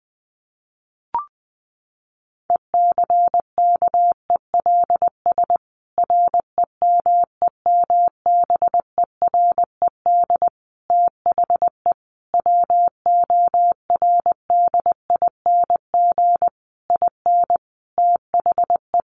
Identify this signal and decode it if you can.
Morse code, heard as a keyed tone, about 20 wpm, 700 Hz. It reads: ECKELS REMEMBERED THE WORDING IN THE
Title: A Sound Of Thunder 20wpm